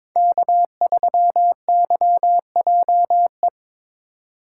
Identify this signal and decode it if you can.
Morse code, heard as a keyed tone, about 22 wpm, 700 Hz. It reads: K3YJE
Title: USA Call Signs - 4x Character Spacing - 22wpm